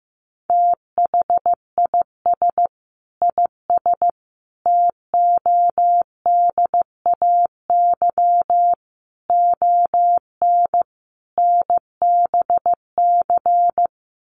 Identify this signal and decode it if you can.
Morse code, heard as a keyed tone, about 15 wpm, 700 Hz. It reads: THIS IS TODAY ON NBC
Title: Sets of 5 Words Encore 15wpm